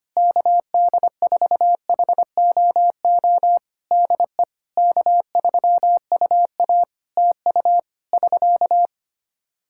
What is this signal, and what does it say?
Morse code, heard as a keyed tone, about 25 wpm, 700 Hz. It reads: KD4HOO DE K3UA TU <SK>
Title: Practice QSO - Exchange 4 - Ending - 25wpm